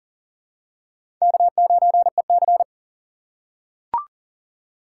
Morse code, heard as a keyed tone, about 40 wpm, 700 Hz. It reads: K9EC
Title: USA Call Signs - 2x Character Spacing - 40wpm